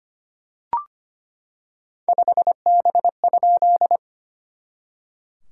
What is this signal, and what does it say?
Morse code, heard as a keyed tone, about 25 wpm, 690 Hz. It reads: 5B?
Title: SOTA QSO Practice 25wpm